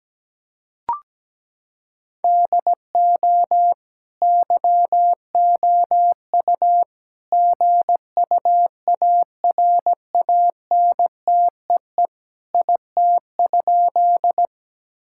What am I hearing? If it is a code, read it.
Morse code, heard as a keyed tone, about 17 wpm, 700 Hz. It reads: DO YOU GUARANTEE IT?
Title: Sentences from Top 1600 Words 17wpm